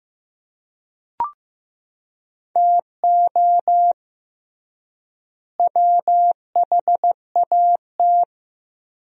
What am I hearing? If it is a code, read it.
Morse code, heard as a keyed tone, about 15 wpm, 700 Hz. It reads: TO WHAT
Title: 3x Word-Spacing - Sentences from Top 200 Words - 15wpm